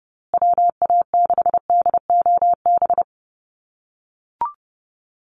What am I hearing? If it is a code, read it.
Morse code, heard as a keyed tone, about 30 wpm, 700 Hz. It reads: WA6DOB
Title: USA Call Signs Repeated 2x 30wpm